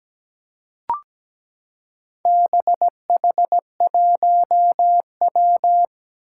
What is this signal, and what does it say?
Morse code, heard as a keyed tone, about 17 wpm, 700 Hz. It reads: BH1W
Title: Four Challenging Letter-Number Combinations 17wpm